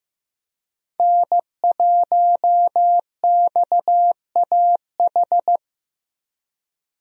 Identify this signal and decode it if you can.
Morse code, heard as a keyed tone, about 15 wpm, 700 Hz. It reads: N1XAH
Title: USA Call Signs - 5 Characters - Repeated 2x - 15wpm